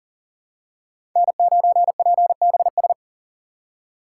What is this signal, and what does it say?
Morse code, heard as a keyed tone, about 40 wpm, 700 Hz. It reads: N9PBS
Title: USA Call Signs - 5 Characters - 6x Character Spacing - 40wpm